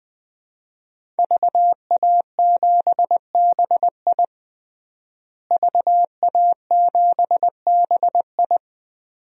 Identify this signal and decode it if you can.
Morse code, heard as a keyed tone, about 20 wpm, 700 Hz. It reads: VA7BI VA7BI
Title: DXCC-100 Call Signs - 5 Characters - Repeated 2x - 20wpm